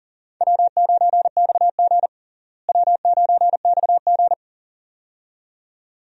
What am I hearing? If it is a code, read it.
Morse code, heard as a keyed tone, about 37 wpm, 700 Hz. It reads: W9XG W9XG
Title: USA Call Signs - 4 Characters - Repeated 2x - 40wpm